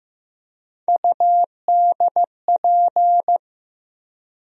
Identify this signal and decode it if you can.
Morse code, heard as a keyed tone, about 15 wpm, 700 Hz. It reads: UDP
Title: USA Call Sign Postfixes - 3 Letters 15wpm